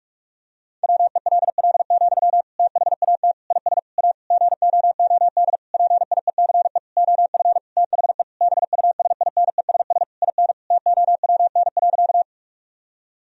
Morse code, heard as a keyed tone, about 46 wpm, 700 Hz. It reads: WELL, THAT IS A GOOD PIECE OF THE BUSINESS IN TOWN.
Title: Sentences from Top 1000 Words - Review Part 2 - 45wpm